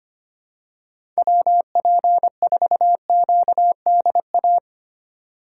Morse code, heard as a keyed tone, about 25 wpm, 700 Hz. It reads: WP4QDA
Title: USA Call Signs - 6 Characters - 2x Character Spacing - 25wpm